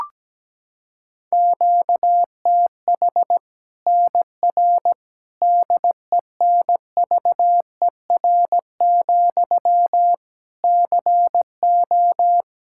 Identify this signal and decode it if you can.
Morse code, heard as a keyed tone, about 17 wpm, 700 Hz. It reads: QTH NR DENVER, CO
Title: K1USN SST QSO Practice 17wpm